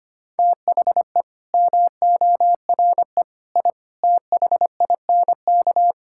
Morse code, heard as a keyed tone, about 25 wpm, 700 Hz. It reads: THE MORE I THINK
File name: Sets of 9 Words - 25wpm